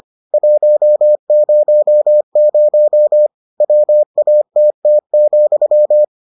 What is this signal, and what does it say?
Morse code, heard as a keyed tone, about 25 wpm, 600 Hz. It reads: 100 WATT,